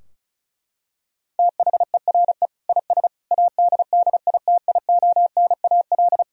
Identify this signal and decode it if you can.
Morse code, heard as a keyed tone, about 34 wpm, 700 Hz. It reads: THERE IS ADDITIONAL
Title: Sentences from Top 1000 Words - Review Part 4 - 35wpm